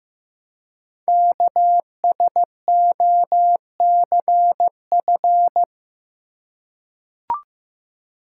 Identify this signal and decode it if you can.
Morse code, heard as a keyed tone, about 15 wpm, 700 Hz. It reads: KSOCF
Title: Code Groups - 5 Letter:Numbers - 15wpm